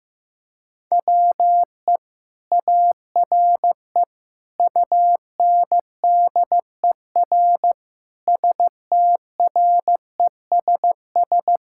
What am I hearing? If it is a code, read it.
Morse code, heard as a keyed tone, about 16 wpm, 700 Hz. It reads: WE ARE UNDER STRESS
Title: Sentences from Top 1000 Words - Review Part 6 - 15wpm